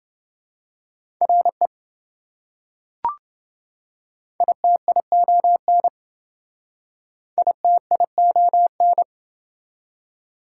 Morse code, heard as a keyed tone, about 27 wpm, 700 Hz. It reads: RE ITION ITION